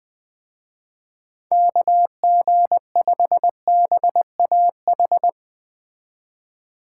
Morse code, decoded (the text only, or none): KG5BAH